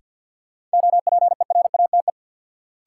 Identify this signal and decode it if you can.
Morse code, heard as a keyed tone, about 52 wpm, 700 Hz. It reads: OPERATE
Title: Sentences from Top 1500 Words - New Word Drill - 50wpm